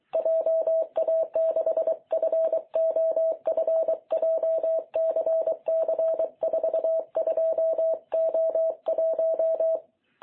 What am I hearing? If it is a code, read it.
Morse code, heard as a keyed tone, about 24 wpm, 660 Hz. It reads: JA6FOF JCC42O1
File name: jh8cbh / 2021cw